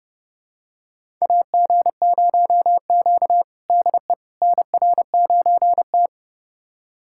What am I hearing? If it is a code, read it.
Morse code, heard as a keyed tone, about 31 wpm, 700 Hz. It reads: AG0Q DE NR9T
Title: USA Callsign DE Callsign - 4 Characters - 30wpm